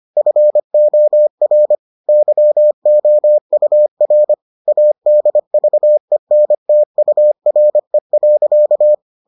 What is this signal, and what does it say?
Morse code, heard as a keyed tone, about 25 wpm, 600 Hz. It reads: FOR YOUR ADVENTURE.